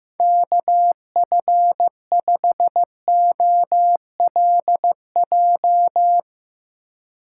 Morse code, heard as a keyed tone, about 15 wpm, 700 Hz. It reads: KF5OLJ
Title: USA Call Signs - 6 Characters - 4x Character Spacing - 15wpm